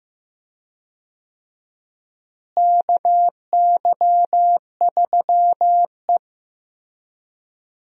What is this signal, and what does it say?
Morse code, heard as a keyed tone, about 16 wpm, 700 Hz. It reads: KY3E
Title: USA Call Signs - 4 Characters - Repeated 2x - 15wpm